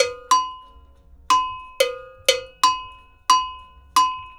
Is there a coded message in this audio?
No.